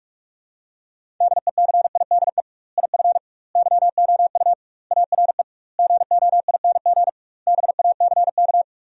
Morse code, heard as a keyed tone, about 42 wpm, 700 Hz. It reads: DECIDE IF YOU ARE GOING BACK
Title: 2x Word-Spacing - Sentences from Top 500 Words - 45wpm